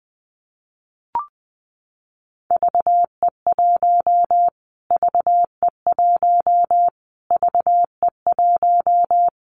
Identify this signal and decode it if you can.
Morse code, heard as a keyed tone, about 20 wpm, 700 Hz. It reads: VE1 VE1 VE1
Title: Callsign Prefixes for Top 25 DXCC - Repeated 3x - 20wpm